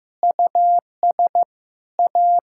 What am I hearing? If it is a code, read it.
Morse code, heard as a keyed tone, about 15 wpm, 700 Hz. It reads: US A
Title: Sentences from Top 1000 Words - Review Part 5 - 15wpm